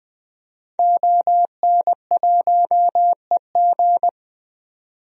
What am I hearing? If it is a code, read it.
Morse code, heard as a keyed tone, about 20 wpm, 700 Hz. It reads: ON1EG
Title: DXCC-100 Call Signs - 5 Characters - 20wpm